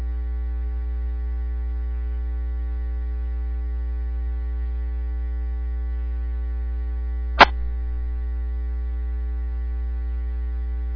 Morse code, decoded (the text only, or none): none